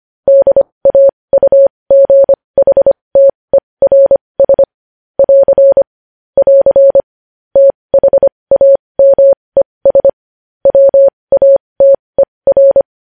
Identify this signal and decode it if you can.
Morse code, heard as a keyed tone, about 25 wpm, spaced as for 20 wpm, 550 Hz. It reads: DAUGHTERS <AR> <AR> THAMES WATER
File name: podcast / News headlines-1x-25x20